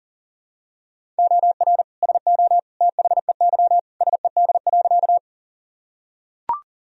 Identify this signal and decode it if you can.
Morse code, heard as a keyed tone, about 40 wpm, 700 Hz. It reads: OR SO THEY SED.
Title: Sentences from Top 100 Words and Abbreviations 40wpm